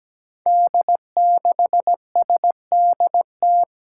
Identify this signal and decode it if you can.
Morse code, heard as a keyed tone, about 17 wpm, 700 Hz. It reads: D6SDT
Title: Five Challenging Letter-Number Combinations 17wpm